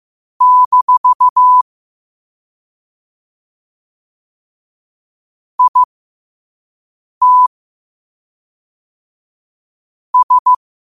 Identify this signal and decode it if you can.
Morse code, heard as a keyed tone, about 15 wpm, spaced as for 5 wpm, 1000 Hz. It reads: - IT S
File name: qotdfiles / 5wpm